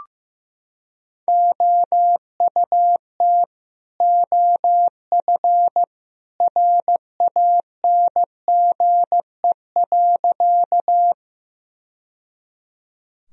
Morse code, heard as a keyed tone, about 16 wpm, 700 Hz. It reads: OUT OF RANGE.
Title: Sentences from Top 1000 Words - Review Part 4 - 15wpm